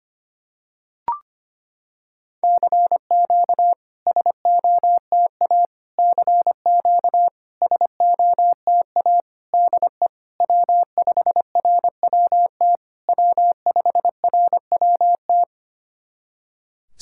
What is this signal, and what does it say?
Morse code, heard as a keyed tone, about 25 wpm, 700 Hz. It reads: CQ SOTA CQ SOTA DE W5RWT W5RWT
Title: SOTA QSO Practice 25wpm